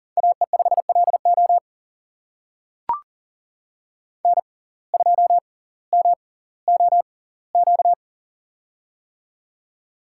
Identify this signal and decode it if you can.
Morse code, heard as a keyed tone, about 40 wpm, spaced as for 20 wpm, 700 Hz. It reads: AE5LO N2MOQ
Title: USA Call Signs - 5 Characters - 6x Character Spacing - 40wpm